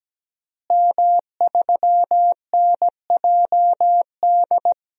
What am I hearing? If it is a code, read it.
Morse code, heard as a keyed tone, about 17 wpm, 700 Hz. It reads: M3NJD
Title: DXCC-100 Call Signs - 5 Characters - Repeated 2x - 17wpm